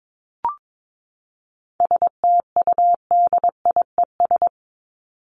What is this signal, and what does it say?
Morse code, heard as a keyed tone, about 22 wpm, 700 Hz. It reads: STUDIES
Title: ES:S Suffix - Top 100 Words with Base Words 22wpm